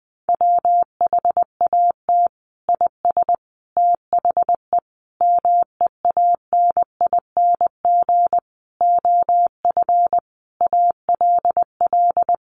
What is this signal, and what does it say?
Morse code, heard as a keyed tone, about 20 wpm, 700 Hz. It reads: WHAT IS THE MEANING OF ALL
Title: Sentences from Top 900 Words 20wpm